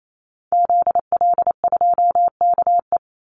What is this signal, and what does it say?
Morse code, heard as a keyed tone, about 28 wpm, 700 Hz. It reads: ZL2KE
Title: Call Signs Repeated 3x - 28wpm